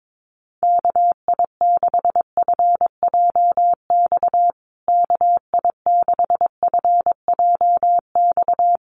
Morse code, heard as a keyed tone, about 22 wpm, 700 Hz. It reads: KI6FJX KI6FJX
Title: USA Call Signs Repeated 3x 22wpm